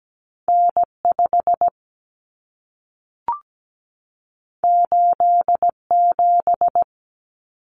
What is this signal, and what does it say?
Morse code, heard as a keyed tone, about 17 wpm, 700 Hz. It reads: N5 87
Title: Common Cut Numbers - 2-Digit 17wpm